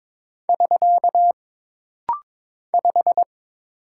Morse code, heard as a keyed tone, about 22 wpm, 700 Hz. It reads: <SK> 5
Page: This rapid-fire timing is geared to push your ICR (Instant Character Recognition) skills.